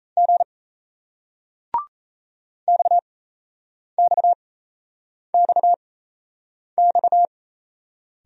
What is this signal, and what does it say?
Morse code, heard as a keyed tone, about 37 wpm, spaced as for 8 wpm, 700 Hz. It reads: G XXXX